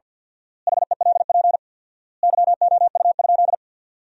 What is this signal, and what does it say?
Morse code, heard as a keyed tone, about 50 wpm, 700 Hz. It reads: HELP YOU?
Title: 4x Word-Spacing - Sentences from Top 200 Words - 50wpm